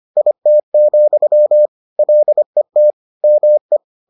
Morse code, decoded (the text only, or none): IT, LET ME